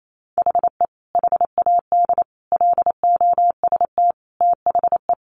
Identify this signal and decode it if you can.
Morse code, heard as a keyed tone, about 28 wpm, 700 Hz. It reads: HE HAD LOST THE
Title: Sentences from Top 1500 Words 28wpm